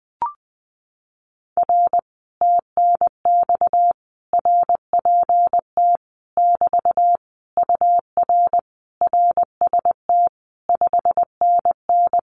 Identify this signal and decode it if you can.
Morse code, heard as a keyed tone, about 20 wpm, 700 Hz. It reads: R TNX RPT <BT> UR RST 5NN
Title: Common QSO Phrases Encore 20wpm